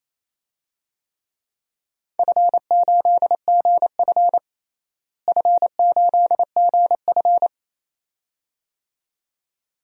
Morse code, heard as a keyed tone, about 28 wpm, 700 Hz. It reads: F8GF F8GF
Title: DXCC-100 Call Signs - 4 Characters - Repeated 2x - 28wpm